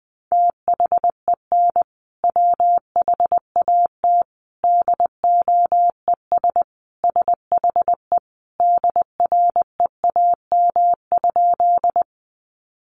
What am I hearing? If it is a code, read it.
Morse code, heard as a keyed tone, about 20 wpm, 700 Hz. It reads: THEN WHAT DOES SHE DREAM?
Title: Sentences from Top 1500 Words 20wpm